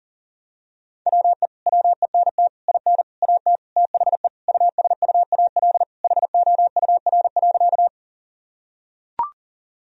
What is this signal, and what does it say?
Morse code, heard as a keyed tone, about 40 wpm, 700 Hz. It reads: WE WENT IN AT THE USUAL HOUR.
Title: Sentences from Top 1000 Words - Review Part 2 - 40wpm